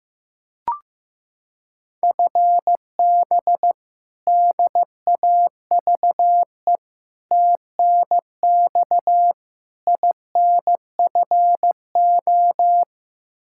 Morse code, heard as a keyed tone, about 15 wpm, 700 Hz. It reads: FB DAVE TNX INFO
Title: K1USN SST QSO Practice 15wpm